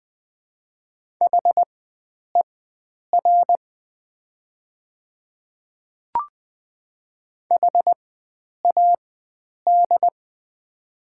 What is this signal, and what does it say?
Morse code, heard as a keyed tone, about 20 wpm, spaced as for 9 wpm, 700 Hz. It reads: HER HAD
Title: 4x Character Spacing - Top 100 Words - 20wpm